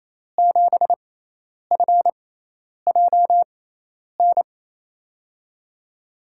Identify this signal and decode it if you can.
Morse code, heard as a keyed tone, about 28 wpm, spaced as for 10 wpm, 700 Hz. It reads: 7FJN
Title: USA Call Signs - 6 Characters - 6x Character Spacing - 28wpm